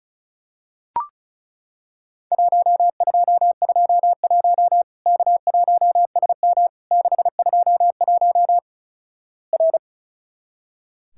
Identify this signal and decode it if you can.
Morse code, heard as a keyed tone, about 36 wpm, 700 Hz. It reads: 1221 K1SM 621 R
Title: Worked All Europe DX Contest - QTC Exchange - 35wpm